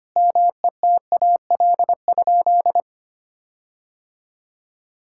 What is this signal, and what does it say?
Morse code, heard as a keyed tone, about 25 wpm, 700 Hz. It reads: METAL?
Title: Sentences from Top 1600 Words 25wpm